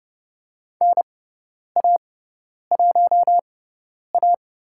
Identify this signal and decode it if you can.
Morse code, heard as a keyed tone, about 30 wpm, spaced as for 10 wpm, 700 Hz. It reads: NA1A
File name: Lesson 17g - Letter u - Call Signs 30wpm